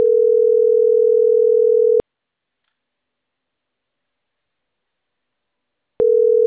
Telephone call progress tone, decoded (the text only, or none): ringback tone